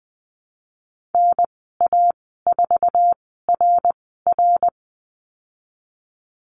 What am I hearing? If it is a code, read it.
Morse code, heard as a keyed tone, about 20 wpm, spaced as for 14 wpm, 700 Hz. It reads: NA4RR
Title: 088 Lesson 15g - Number 5 - Call Signs 20wpm